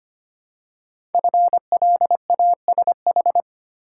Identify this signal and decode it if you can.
Morse code, heard as a keyed tone, about 25 wpm, 700 Hz. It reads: FLASH
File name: 5 Letter Words - First 3 Letters Given - 25wpm